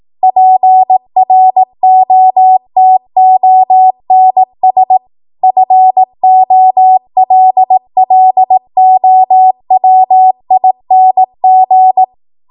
Morse code, heard as a keyed tone, about 18 wpm, 750 Hz. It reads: PROTONS FOLLOWING
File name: Morse / 18WPM